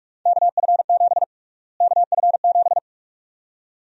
Morse code, heard as a keyed tone, about 42 wpm, 700 Hz. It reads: KF7 KF7
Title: USA Call Sign Prefixes - Repeated 2x - 45wpm